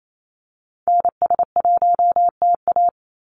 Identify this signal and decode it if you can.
Morse code, heard as a keyed tone, about 28 wpm, 700 Hz. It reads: NS1TA
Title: NA SOTA Activator Callsigns 28wpm